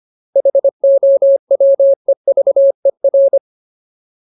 Morse code, heard as a keyed tone, about 25 wpm, 550 Hz. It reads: HOWEVER
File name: Ultimate Word List / 25WPM Ultimate